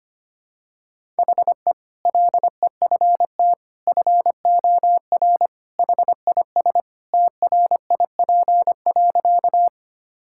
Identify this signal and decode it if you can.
Morse code, heard as a keyed tone, about 25 wpm, 700 Hz. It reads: HE LEFT FOR HIS TRIP.
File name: practice / 25wpm / Sentences with 2-Word Phrases - 25wpm